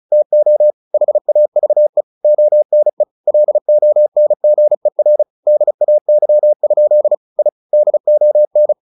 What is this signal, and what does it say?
Morse code, heard as a keyed tone, about 34 wpm, 600 Hz. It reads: TO HAVE ONE LONGER DAY? I DON
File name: the old man and the sea / the old man and the sea 1 35wpm